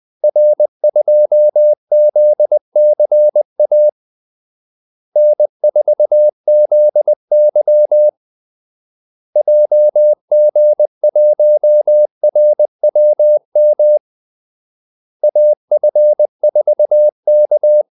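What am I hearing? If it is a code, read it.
Morse code, heard as a keyed tone, about 20 wpm, 600 Hz. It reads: R2ZCA N4ZY JG1RWM AF4K
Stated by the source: Morse call-sign practice at 20 wpm.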